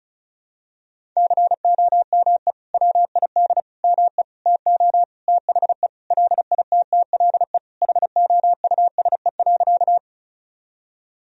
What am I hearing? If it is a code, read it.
Morse code, heard as a keyed tone, about 34 wpm, 700 Hz. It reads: COME WID ME TO THE LITTLE HOUSE.
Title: Sentences from Top 200 Words and Abbreviations 35wpm